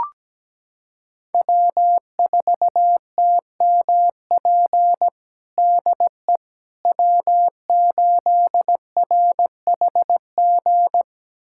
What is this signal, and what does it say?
Morse code, heard as a keyed tone, about 17 wpm, 700 Hz. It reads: W4TMP DE W8RHG
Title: USA Callsign DE Callsign - 5 Characters - 17wpm